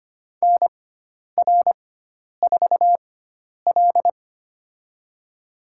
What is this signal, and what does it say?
Morse code, heard as a keyed tone, about 25 wpm, spaced as for 10 wpm, 700 Hz. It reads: NR4L
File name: Lesson 16g - Letter c - Call Signs 25wpm